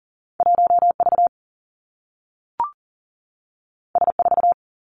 Morse code, heard as a keyed tone, about 40 wpm, 700 Hz. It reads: 1 V S 4
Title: Two Challenging Letter-Number Combinations 40wpm